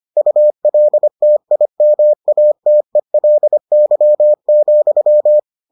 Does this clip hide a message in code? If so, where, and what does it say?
Yes: Morse code from 0.2 to 5.4 s, heard as a keyed tone, about 25 wpm, 600 Hz. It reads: ULTIMATELY,